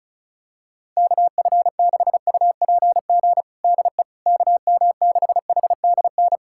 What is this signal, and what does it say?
Morse code, heard as a keyed tone, about 36 wpm, 700 Hz. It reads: KF6UPG DE KM6HDN
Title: USA Callsign DE Callsign - 6 Characters - 35wpm